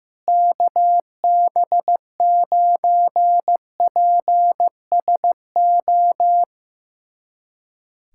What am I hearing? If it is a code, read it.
Morse code, heard as a keyed tone, about 15 wpm, 700 Hz. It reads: KB9PSO
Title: USA Callsign DE Callsign 15wpm